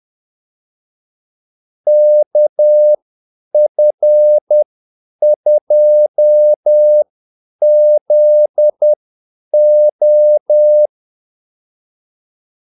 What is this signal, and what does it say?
Morse code, heard as a keyed tone, about 10 wpm, spaced as for 8 wpm, 600 Hz. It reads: KF2ZO
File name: Callsigns-10-8wpm